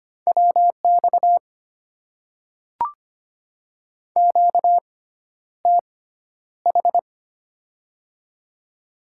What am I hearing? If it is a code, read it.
Morse code, heard as a keyed tone, about 25 wpm, spaced as for 11 wpm, 700 Hz. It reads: WX QTH
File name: Top 90 QSO Elements - 6x Character Spacing - 25wpm